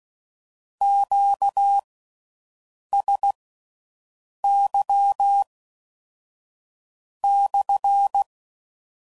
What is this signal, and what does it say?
Morse code, heard as a keyed tone, about 16 wpm, spaced as for 6 wpm, 800 Hz. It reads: QSY /